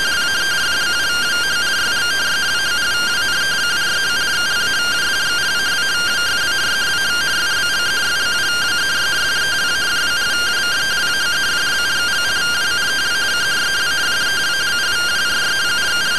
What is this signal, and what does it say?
Two-tone teleprinter signal, about 170 Hz shift.